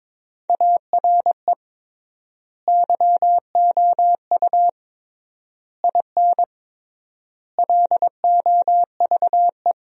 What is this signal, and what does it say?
Morse code, heard as a keyed tone, about 22 wpm, 700 Hz. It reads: ARE YOU IN LOVE
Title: 3x Word-Spacing - Sentences from Top 400 Words - 22wpm